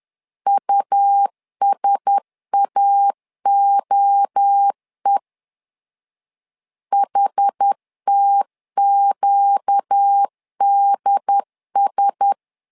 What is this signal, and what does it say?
Morse code, heard as a keyed tone, about 10 wpm, 800 Hz. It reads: USAOE HTQDS